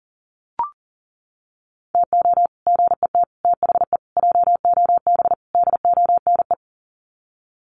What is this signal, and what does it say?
Morse code, heard as a keyed tone, about 40 wpm, 700 Hz. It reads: TO GET THE JOB DONE
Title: Sets of 5 Words Encore2 40wpm